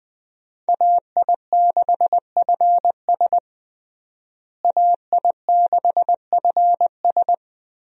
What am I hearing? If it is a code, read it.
Morse code, heard as a keyed tone, about 20 wpm, 700 Hz. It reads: AI6FS AI6FS
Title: USA Call Signs - 5 Characters - Repeated 3x - 20wpm